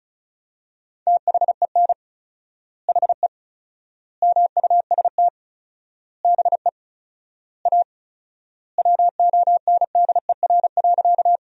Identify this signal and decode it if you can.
Morse code, heard as a keyed tone, about 36 wpm, 700 Hz. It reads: THEN HE MUST BE A WONDER.
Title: 4x Word-Spacing - Sentences from Top 500 Words - 35wpm